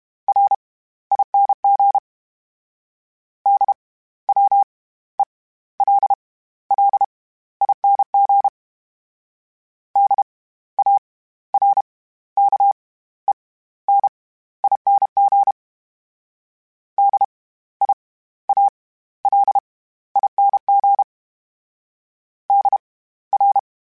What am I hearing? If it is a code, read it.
Morse code, heard as a keyed tone, about 32 wpm, spaced as for 16 wpm, 800 Hz. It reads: RING DWELLING DARKENING DIALING DR
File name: Special INGs / Special INGs-mp3s / SlowToFaster ing @30